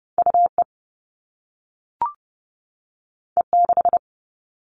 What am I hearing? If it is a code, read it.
Morse code, heard as a keyed tone, about 30 wpm, 700 Hz. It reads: UE E6
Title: All Cut Numbers - 2-Digit 30wpm